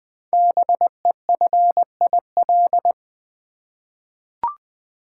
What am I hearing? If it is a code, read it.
Morse code, heard as a keyed tone, about 20 wpm, 700 Hz. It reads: BEFIL
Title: Code Groups - 5 Letters - 20wpm